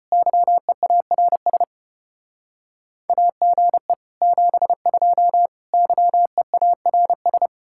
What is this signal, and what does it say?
Morse code, heard as a keyed tone, about 30 wpm, 700 Hz. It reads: YEARS AGE 72 YEARS
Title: Speed-Racing Common QSO Phrases - 25wpm